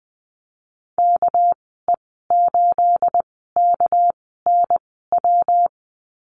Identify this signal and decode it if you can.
Morse code, heard as a keyed tone, about 20 wpm, spaced as for 15 wpm, 700 Hz. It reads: KE8KNW
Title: 213 Lesson 34g - Number 8 - Call Signs 20wpm